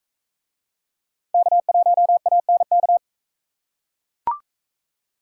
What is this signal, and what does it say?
Morse code, heard as a keyed tone, about 40 wpm, 700 Hz. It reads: K1ANK